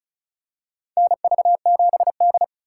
Morse code, heard as a keyed tone, about 36 wpm, 700 Hz. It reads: NV7D
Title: USA Call Signs - 4 Characters - 2x Character Spacing - 35wpm